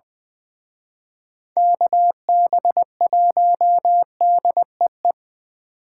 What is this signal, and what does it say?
Morse code, heard as a keyed tone, about 24 wpm, 700 Hz. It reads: KB1DEE
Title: Speed Racing - USA 2x3 Call Signs 20wpm